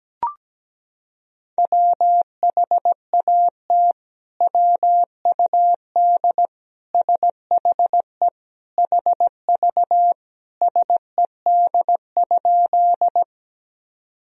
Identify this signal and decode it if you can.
Morse code, heard as a keyed tone, about 17 wpm, 700 Hz. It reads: WHAT WUD SHE HV SED?
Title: Sentences from Top 100 Words and Abbreviations 17wpm